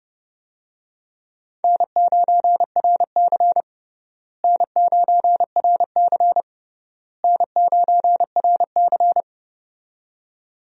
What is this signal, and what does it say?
Morse code, heard as a keyed tone, about 30 wpm, 700 Hz. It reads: N9RC N9RC N9RC
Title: USA Call Signs - 4 Characters - Repeated 3x - 30wpm